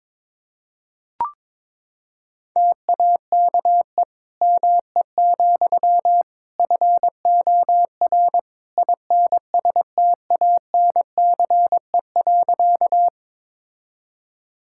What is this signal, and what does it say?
Morse code, heard as a keyed tone, about 22 wpm, 700 Hz. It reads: TAKE ME, FOR INSTANCE.
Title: Sentences from Top 1000 Words - Review Part 5 - 22wpm